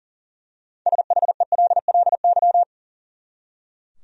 Morse code, heard as a keyed tone, about 40 wpm, 700 Hz. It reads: SHELLY
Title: USA Common Names - 40wpm